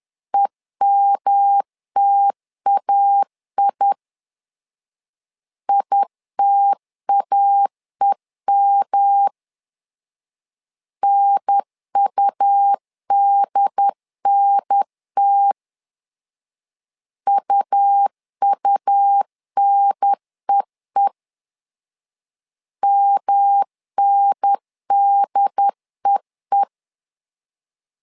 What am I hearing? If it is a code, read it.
Morse code, heard as a keyed tone, about 11 wpm, 800 Hz. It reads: EMTAI ITAEM NUDNT UUNEE MNDEE
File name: LEZIONI / 3 MWRO / 3 M 10wpm 1828